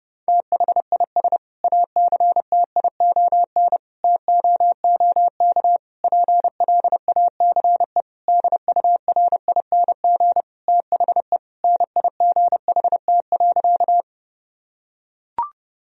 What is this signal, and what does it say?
Morse code, heard as a keyed tone, about 30 wpm, 700 Hz. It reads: THIS ACTION TOOK PLACE DURING THE NIGHT.
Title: Sentences from Top 1000 Words - Review Part 4 - 30wpm